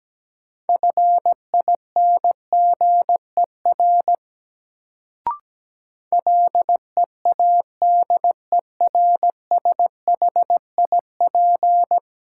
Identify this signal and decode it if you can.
Morse code, heard as a keyed tone, about 17 wpm, 700 Hz. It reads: FINGER LEADERSHIP
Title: Sentences from Top 1600 Words - New Word Drill - 17wpm